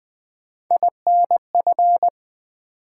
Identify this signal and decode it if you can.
Morse code, heard as a keyed tone, about 20 wpm, 700 Hz. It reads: INF
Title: Build A Word - 3 Letters At A Time - 9-Letter Words - 20wpm